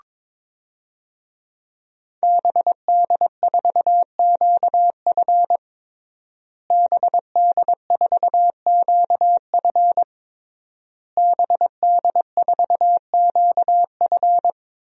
Morse code, heard as a keyed tone, about 22 wpm, 700 Hz. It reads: BD4QF BD4QF BD4QF
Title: DXCC-100 Call Signs - 5 Characters - Repeated 3x - 22wpm